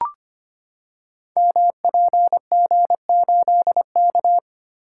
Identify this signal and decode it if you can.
Morse code, heard as a keyed tone, about 25 wpm, 700 Hz. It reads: MPG8K
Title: Code Groups - 5 Letter:Numbers - 25wpm